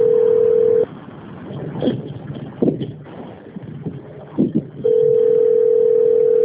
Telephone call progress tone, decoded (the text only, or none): ringback tone